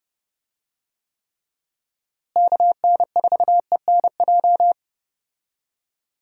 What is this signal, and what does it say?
Morse code, heard as a keyed tone, about 30 wpm, 700 Hz. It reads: KN4ENJ
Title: USA Call Signs - 6 Characters - Repeated 2x - 30wpm